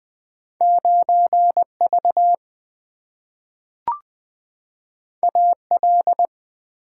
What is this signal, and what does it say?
Morse code, heard as a keyed tone, about 20 wpm, 700 Hz. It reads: 9 V A L
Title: Two Letter-Number Combinations - 20wpm